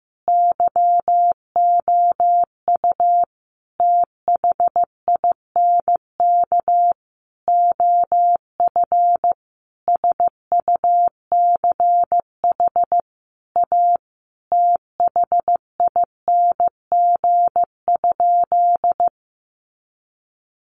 Morse code, heard as a keyed tone, about 15 wpm, 700 Hz. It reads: YOU THINK OF SUCH A THING?
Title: Sentences from Top 200 Words 15wpm